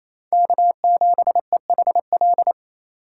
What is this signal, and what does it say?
Morse code, heard as a keyed tone, about 28 wpm, 700 Hz. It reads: K7EHL
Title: USA Call Signs - 5 Characters - 8x Character Spacing - 28wpm